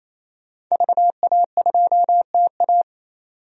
Morse code, heard as a keyed tone, about 28 wpm, 700 Hz. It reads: VA2TA
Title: DXCC-100 Call Signs - 5 Characters - 28wpm